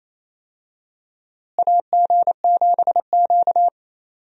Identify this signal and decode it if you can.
Morse code, heard as a keyed tone, about 28 wpm, 700 Hz. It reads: AG7Q
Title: USA Call Signs - 4 Characters - Repeated 2x - 28wpm